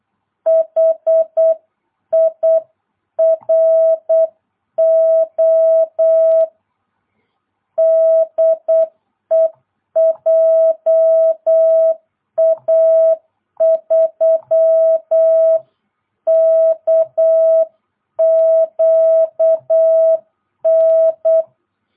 Morse code, heard as a keyed tone, about 8 wpm, 650 Hz. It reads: HIRO DE JA3KQN